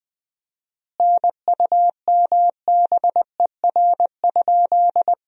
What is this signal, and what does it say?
Morse code, heard as a keyed tone, about 21 wpm, 700 Hz. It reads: NUMBER?
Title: 4x Word-Spacing - Sentences from Top 100 Words - 20wpm